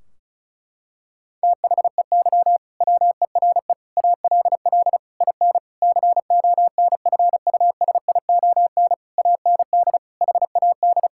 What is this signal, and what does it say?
Morse code, heard as a keyed tone, about 34 wpm, 700 Hz. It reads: THEY WERE ALL IN CONFUSION AND HAD
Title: Sentences from Top 1600 Words 35wpm